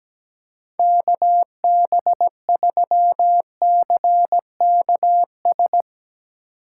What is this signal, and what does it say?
Morse code, heard as a keyed tone, about 19 wpm, 700 Hz. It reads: KB3CKS
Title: Speed Racing - USA 2x3 Call Signs 17wpm